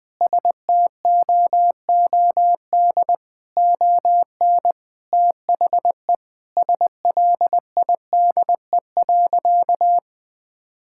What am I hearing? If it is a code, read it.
Morse code, heard as a keyed tone, about 20 wpm, 700 Hz. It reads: STOOD ON THE SLIDE.
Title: Sentences from Top 1600 Words 20wpm